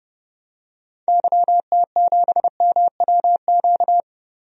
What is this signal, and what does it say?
Morse code, heard as a keyed tone, about 30 wpm, 700 Hz. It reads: YT7MWQ
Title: DXCC-100 Call Signs - 6 Characters - Repeated 3x - 30wpm